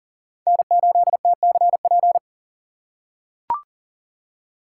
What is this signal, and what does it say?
Morse code, heard as a keyed tone, about 40 wpm, 700 Hz. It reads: N8TCP
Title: USA Call Signs - 5 Characters - 6x Character Spacing - 40wpm